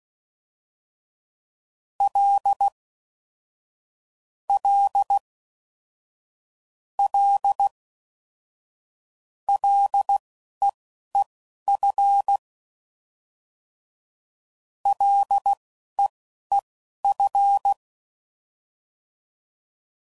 Morse code, heard as a keyed tone, about 16 wpm, spaced as for 11 wpm, 800 Hz. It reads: L L L LEEF LEEF
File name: Basic Sets / QST de WA7RSO / Combnations